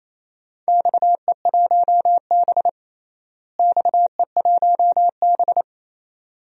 Morse code, heard as a keyed tone, about 28 wpm, 700 Hz. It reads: XE1B XE1B
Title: DXCC-100 Call Signs - 4 Characters - Repeated 2x - 28wpm